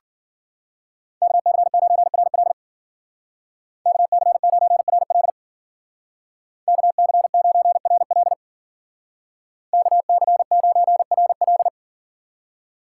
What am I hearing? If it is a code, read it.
Morse code, heard as a keyed tone, about 49 wpm, 700 Hz. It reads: KC9RL KC9RL KC9RL KC9RL
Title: Speed Racing - USA 2x2 Call Signs 40wpm